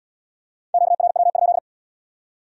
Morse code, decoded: QRW?

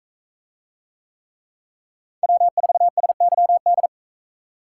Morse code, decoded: W4SYD

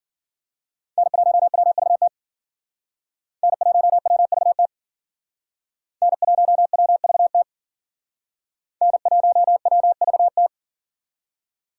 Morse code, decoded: N1WVT N1WVT N1WVT N1WVT

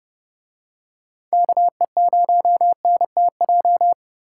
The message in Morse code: KE0NTJ

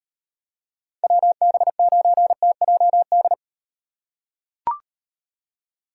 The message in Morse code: WB9TJD

